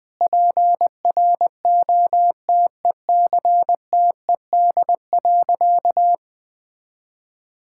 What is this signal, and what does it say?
Morse code, heard as a keyed tone, about 19 wpm, 700 Hz. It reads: PROTECTED.
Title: Sentences with 9-Word Phrases - Spoken Beginning - 20wpm